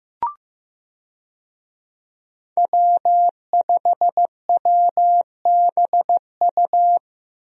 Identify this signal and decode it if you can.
Morse code, heard as a keyed tone, about 15 wpm, 700 Hz. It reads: W5WBU
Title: USA Call Signs - 5 Characters - Repeated 2x - 15wpm